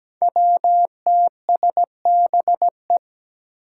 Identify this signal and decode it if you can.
Morse code, heard as a keyed tone, about 17 wpm, 700 Hz. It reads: WTSBE
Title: Five Challenging Letter-Number Combinations 17wpm